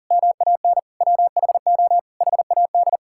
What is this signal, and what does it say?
Morse code, heard as a keyed tone, about 40 wpm, 700 Hz. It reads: MAN WHO HAD
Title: Sentences from Top 1000 Words - Review Part 6 - 40wpm